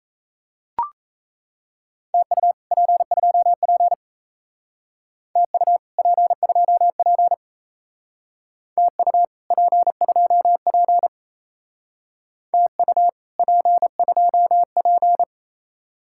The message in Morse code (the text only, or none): TU P2P TU P2P TU P2P TU P2P